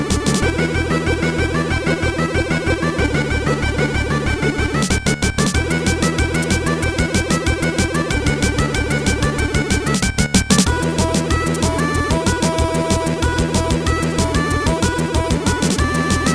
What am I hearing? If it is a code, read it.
There is no code.